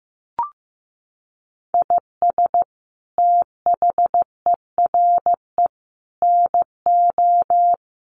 Morse code, heard as a keyed tone, about 15 wpm, 700 Hz. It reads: IS THERE NO
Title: Sentences from Top 1500 Words 15wpm